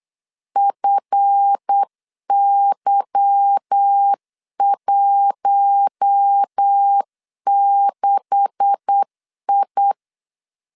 Morse code, heard as a keyed tone, about 8 wpm, 800 Hz. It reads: FY16I
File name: LEZIONI / 9 286 8wpm 3425